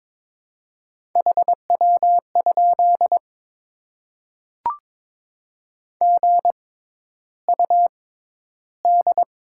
Morse code, heard as a keyed tone, about 22 wpm, spaced as for 11 wpm, 700 Hz. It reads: HW? GUD
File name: Top 20 QSO Elements - 6x Character Spacing - 22wpm